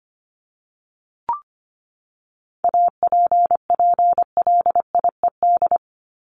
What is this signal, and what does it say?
Morse code, heard as a keyed tone, about 25 wpm, 700 Hz. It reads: APPLIED
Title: ED Suffix - Top 100 Words with Base Words 25wpm